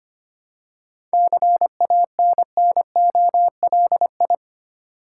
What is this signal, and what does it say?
Morse code, heard as a keyed tone, about 25 wpm, 700 Hz. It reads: CANNOLI
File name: practice / 25wpm / Difficult Words with Hints - 25wpm